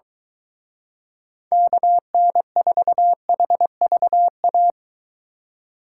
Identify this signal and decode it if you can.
Morse code, heard as a keyed tone, about 23 wpm, 700 Hz. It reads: KN4HVA